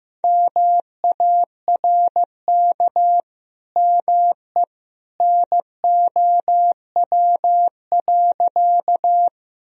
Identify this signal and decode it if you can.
Morse code, heard as a keyed tone, about 15 wpm, 700 Hz. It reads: MARK ME NOW.
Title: Sentences from Top 1000 Words - Review Part 1 - 15wpm